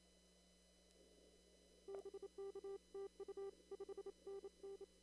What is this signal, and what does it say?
Morse code, heard as a keyed tone, about 27 wpm, 390 Hz. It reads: BK TU 5NN
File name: CW / kt5j wpx cw-op-k5pi